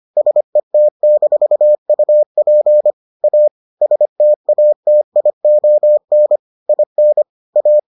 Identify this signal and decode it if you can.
Morse code, heard as a keyed tone, about 25 wpm, 600 Hz. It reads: SET-UP A STATION IN A